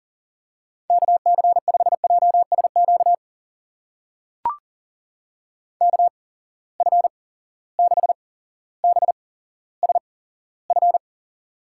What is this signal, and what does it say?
Morse code, heard as a keyed tone, about 39 wpm, spaced as for 17 wpm, 700 Hz. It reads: KC5JSQ KF6BSF
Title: USA Call Signs - 6 Characters - 8x Character Spacing - 40wpm